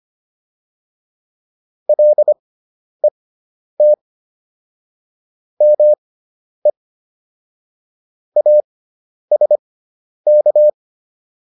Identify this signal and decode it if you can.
Morse code, heard as a keyed tone, about 25 wpm, spaced as for 10 wpm, 600 Hz. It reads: LET ME ASK